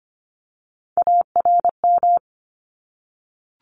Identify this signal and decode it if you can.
Morse code, heard as a keyed tone, about 25 wpm, 700 Hz. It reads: ARM
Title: ES:S Suffix - Top 100 Words with Base Words 25wpm